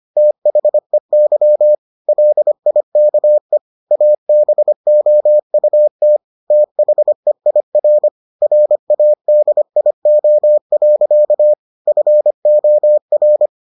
Morse code, heard as a keyed tone, about 25 wpm, 600 Hz. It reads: THEY LIKE ABOUT THEIR RADIO. FOR